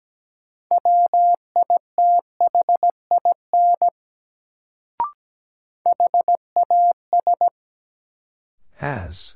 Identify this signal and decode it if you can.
Morse code, heard as a keyed tone, about 17 wpm, 700 Hz. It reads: WITHIN HAS
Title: Sets of 2 Words - New Word Drill - 17wpm